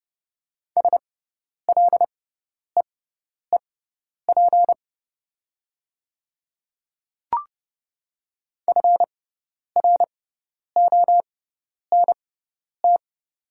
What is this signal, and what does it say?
Morse code, heard as a keyed tone, about 30 wpm, spaced as for 10 wpm, 690 Hz. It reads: SLEEP FRONT
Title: 6x Character Spacing - Top 500 Words - 30wpm